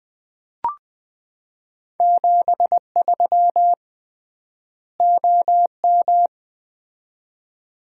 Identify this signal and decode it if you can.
Morse code, heard as a keyed tone, about 20 wpm, 700 Hz. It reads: 73 OM
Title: Common QSO Phrases - 3x Word-Spacing - 20wpm